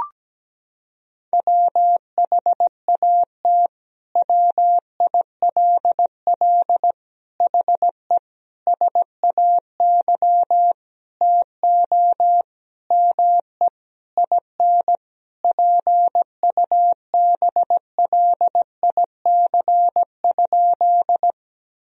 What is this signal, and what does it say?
Morse code, heard as a keyed tone, about 17 wpm, 700 Hz. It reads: WHAT WILL HE SAY TO ME IN PUBLIC?
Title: Sentences from Top 1000 Words - Review Part 3 - 17wpm